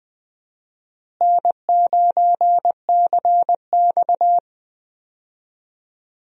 Morse code, heard as a keyed tone, about 21 wpm, 700 Hz. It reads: N9CX
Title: USA Call Signs - 4 Characters - Repeated 2x - 20wpm